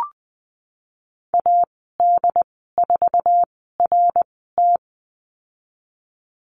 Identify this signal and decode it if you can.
Morse code, heard as a keyed tone, about 20 wpm, spaced as for 15 wpm, 700 Hz. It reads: AD4RT
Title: 082 Lesson 14g - Number 2 - Call Signs 20wpm